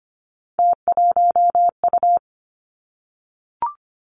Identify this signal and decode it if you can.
Morse code, heard as a keyed tone, about 25 wpm, 700 Hz. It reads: T1U
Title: Three Challenging Letter-Number Combinations 25wpm